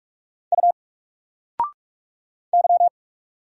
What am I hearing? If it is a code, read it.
Morse code, heard as a keyed tone, about 44 wpm, 700 Hz. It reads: U Y
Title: Single Character (Letter:Number:Punctuation:Prosign) 45wpm